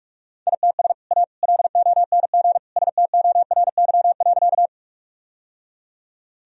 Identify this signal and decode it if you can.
Morse code, heard as a keyed tone, about 44 wpm, 700 Hz. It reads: ITS A LONG STORY.
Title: Sentences from Top 1000 Words - Review Part 1 - 45wpm